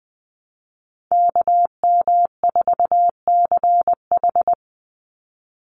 Morse code, heard as a keyed tone, about 20 wpm, 700 Hz. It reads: KM4CH
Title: Call Signs Repeated 2x - 20wpm